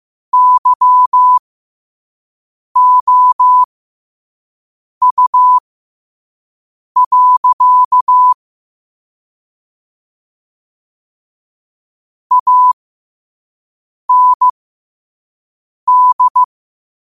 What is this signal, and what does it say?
Morse code, heard as a keyed tone, about 15 wpm, spaced as for 5 wpm, 1000 Hz. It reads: YOU. AND
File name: qotdfiles / 5wpm